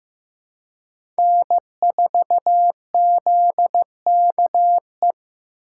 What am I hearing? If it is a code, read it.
Morse code, heard as a keyed tone, about 15 wpm, 700 Hz. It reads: N4ZKE
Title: USA Call Signs - 5 Characters - 4x Character Spacing - 15wpm